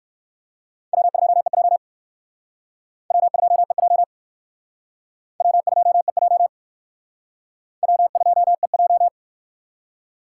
Morse code, heard as a keyed tone, about 54 wpm, 700 Hz. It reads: W2EJ W2EJ W2EJ W2EJ
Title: Speed Racing - USA 1x2 Call Signs 45wpm